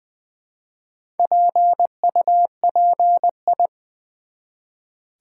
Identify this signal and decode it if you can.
Morse code, heard as a keyed tone, about 20 wpm, 700 Hz. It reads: PUPI
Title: Build A Word - 1 Letter At A Time - 5-Letter Words - 20wpm